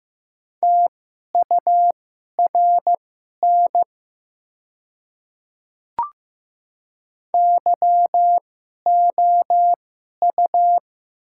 Morse code, heard as a keyed tone, about 15 wpm, spaced as for 11 wpm, 690 Hz. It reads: TURN YOU
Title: 2x Character Spacing - Top 300 Words - 15wpm